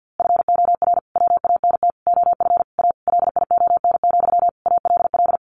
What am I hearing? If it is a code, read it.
Morse code, heard as a keyed tone, about 50 wpm, 700 Hz. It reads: FOR WANT OF A LION, ALL
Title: A Sound Of Thunder 50wpm